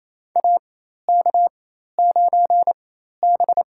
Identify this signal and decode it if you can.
Morse code, heard as a keyed tone, about 28 wpm, spaced as for 13 wpm, 700 Hz. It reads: AK9B
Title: USA Call Signs - 4 Characters - 4x Character Spacing - 28wpm